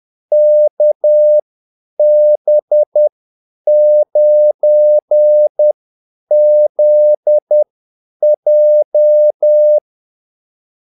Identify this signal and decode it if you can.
Morse code, heard as a keyed tone, about 10 wpm, spaced as for 8 wpm, 600 Hz. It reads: KB9ZJ
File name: Callsigns-10-8wpm